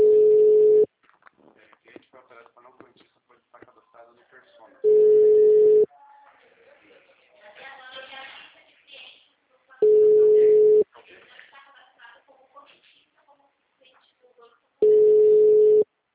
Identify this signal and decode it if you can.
Ringback tone (Europe, 425 Hz), with speech.
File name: out- -2014- - - .9087